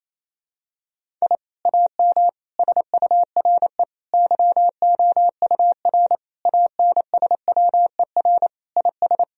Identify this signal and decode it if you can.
Morse code, heard as a keyed tone, about 28 wpm, 700 Hz. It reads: I AM SURE YOUR ANSWER IS
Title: Sentences from Top 1000 Words - Review Part 2 - 28wpm